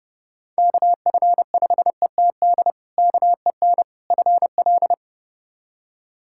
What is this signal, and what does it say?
Morse code, heard as a keyed tone, about 32 wpm, 700 Hz. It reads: KF5ETD KEN FL
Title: K1USN SST QSO Practice - 30wpm